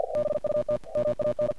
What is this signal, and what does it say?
Morse code, heard as a keyed tone, about 47 wpm, 600 Hz, with noise. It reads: BUT MAN
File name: archived / CRSnet